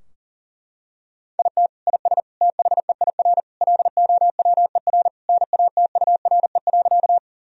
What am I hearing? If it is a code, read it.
Morse code, heard as a keyed tone, about 40 wpm, 700 Hz. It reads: IT IS THEIR LOWER NATURE.
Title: Sentences from Top 1000 Words - Review Part 4 - 40wpm